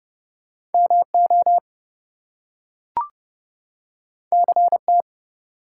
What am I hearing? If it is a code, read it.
Morse code, heard as a keyed tone, about 31 wpm, 700 Hz. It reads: MO CT
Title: US States and Canadian Provinces 30wpm